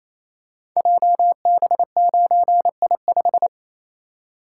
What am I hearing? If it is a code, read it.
Morse code, heard as a keyed tone, about 27 wpm, 700 Hz. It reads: JB9I5
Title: Code Groups - 5 Letter:Numbers - 28wpm